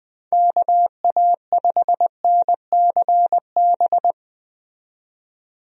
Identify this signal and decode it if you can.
Morse code, heard as a keyed tone, about 20 wpm, 700 Hz. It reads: KA5NCB
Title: USA Call Signs - 6 Characters - 20wpm